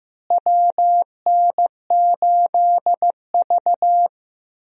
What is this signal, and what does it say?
Morse code, heard as a keyed tone, about 15 wpm, 700 Hz. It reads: WN8V
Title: USA Call Signs - 4 Characters - Repeated 2x - 15wpm